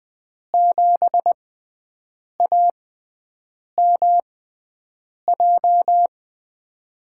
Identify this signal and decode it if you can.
Morse code, heard as a keyed tone, about 20 wpm, spaced as for 7 wpm, 700 Hz. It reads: 7AMJ
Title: USA Call Signs - 6 Characters - 6x Character Spacing - 20wpm